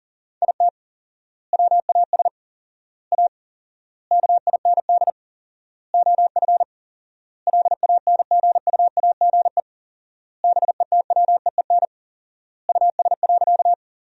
Morse code, heard as a keyed tone, about 40 wpm, 700 Hz. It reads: IT WAS A KIND OF LANGUAGE BETWEEN US.
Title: 4x Word-Spacing - Sentences from Top 500 Words - 40wpm